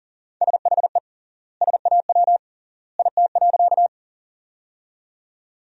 Morse code, heard as a keyed tone, about 40 wpm, 700 Hz. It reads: SHE SAW IT.
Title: 3x Word-Spacing - Sentences from Top 300 Words - 40wpm